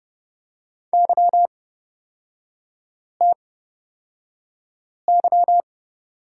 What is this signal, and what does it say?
Morse code, heard as a keyed tone, about 30 wpm, spaced as for 5 wpm, 700 Hz. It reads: YTY
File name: Lesson 36e - BT - Three Characters 30wpm